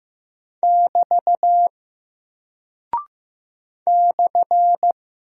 Morse code, heard as a keyed tone, about 15 wpm, 700 Hz. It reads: <BT> /
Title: Single Character (Letter:Number:Punctuation:Prosign) Alt 15wpm